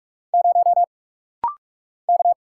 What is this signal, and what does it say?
Morse code, heard as a keyed tone, about 44 wpm, 700 Hz. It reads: 0 K
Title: Single Letter-Number Alt - Rapid-Fire 45wpm